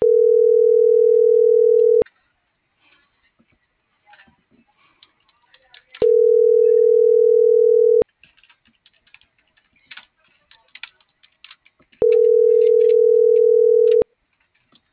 Ringback tone (North America), 440 + 480 Hz.